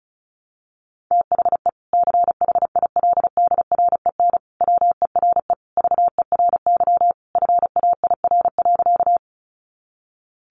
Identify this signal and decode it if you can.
Morse code, heard as a keyed tone, about 34 wpm, 700 Hz. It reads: THE CHILDREN WERE VERY FAIR.
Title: Sentences from Top 1300 Words 35wpm